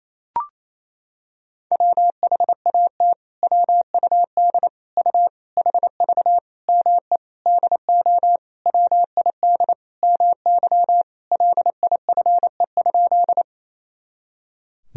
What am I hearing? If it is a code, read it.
Morse code, heard as a keyed tone, about 28 wpm, 700 Hz. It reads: WHAT WUD U HV ME DO WID MY LIFE?
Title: Sentences from Top 300 Words and Abbreviations 28wpm